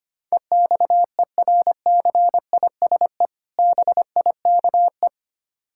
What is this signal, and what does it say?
Morse code, heard as a keyed tone, about 27 wpm, 700 Hz. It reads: EXERCISE BIKE
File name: Easy Words - Speed-Racing with Hints - 25wpm